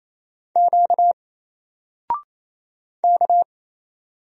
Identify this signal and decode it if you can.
Morse code, heard as a keyed tone, about 27 wpm, 700 Hz. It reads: Q K
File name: Single-Letter-Number-28wpm